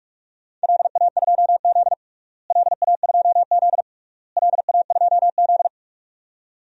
Morse code, heard as a keyed tone, about 41 wpm, 700 Hz. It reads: LA2Z LA2Z LA2Z
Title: DXCC-100 Call Signs - 4 Characters - Repeated 3x - 45wpm